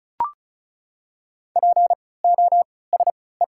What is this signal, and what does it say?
Morse code, heard as a keyed tone, about 34 wpm, spaced as for 20 wpm, 700 Hz. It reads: POSE
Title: 3x Character Spacing - Top 500 Words - 35wpm